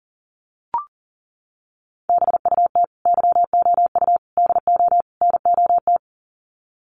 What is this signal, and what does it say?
Morse code, heard as a keyed tone, about 40 wpm, 700 Hz. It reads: BUT YOU DO NOT
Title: Sets of 4 Words 40wpm